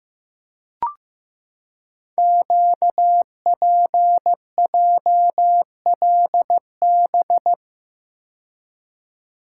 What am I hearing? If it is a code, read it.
Morse code, heard as a keyed tone, about 15 wpm, 700 Hz. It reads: QPJLB
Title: Code Groups - 5 Letters - 15wpm